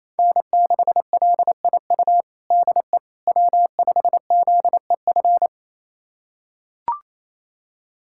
Morse code, heard as a keyed tone, about 28 wpm, 700 Hz. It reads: N6LIU DE W5ZEF
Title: USA Callsign DE Callsign - 5 Characters - 28wpm